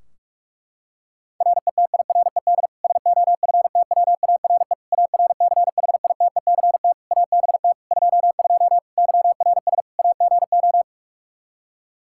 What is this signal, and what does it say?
Morse code, heard as a keyed tone, about 46 wpm, 700 Hz. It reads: RETIRED SOFTWARE ARCHITECT ABT 22 YRS AGO
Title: Practice QSO - Exchange 3 - More Info - 45wpm